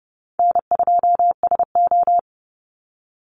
Morse code, heard as a keyed tone, about 30 wpm, 700 Hz. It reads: N2SO
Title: Call Signs - 30wpm